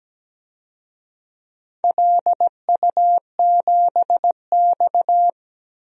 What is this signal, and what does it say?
Morse code, heard as a keyed tone, about 17 wpm, 700 Hz. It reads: LU7X
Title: DXCC-100 Call Signs - 4 Characters - Repeated 3x - 17wpm